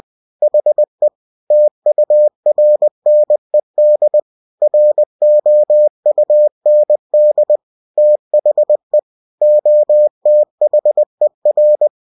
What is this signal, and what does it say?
Morse code, heard as a keyed tone, about 20 wpm, 600 Hz. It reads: HE TURNED ROUND THE OTHER
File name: cw-books / winnie-the-pooh / winnie-20wpm-page-0027